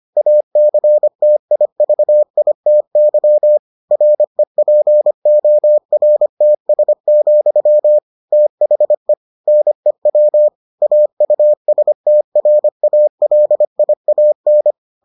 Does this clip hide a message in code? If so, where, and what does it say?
Yes: Morse code from 0.2 to 14.7 s, heard as a keyed tone, about 25 wpm, 600 Hz. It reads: ACTIVITY REPORTS, THE NEW AUSTRALIAN